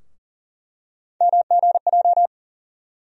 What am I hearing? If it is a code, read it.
Morse code, heard as a keyed tone, about 40 wpm, 700 Hz. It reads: MGJ
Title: USA Call Sign Postfixes - 3 Letters 40wpm